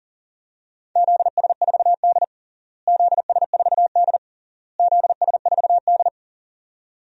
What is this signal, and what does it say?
Morse code, heard as a keyed tone, about 40 wpm, 700 Hz. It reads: ZS4D ZS4D ZS4D
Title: DXCC-100 Call Signs - 4 Characters - Repeated 3x - 40wpm